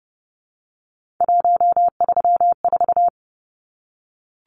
Morse code, heard as a keyed tone, about 30 wpm, 700 Hz. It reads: 134
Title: Numbers - 3-Digit 30wpm